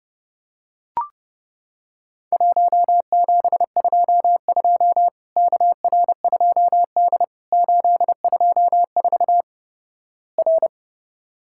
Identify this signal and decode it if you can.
Morse code, heard as a keyed tone, about 32 wpm, 700 Hz. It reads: 1722 KR2D 824 R
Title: Worked All Europe DX Contest - QTC Exchange - 30wpm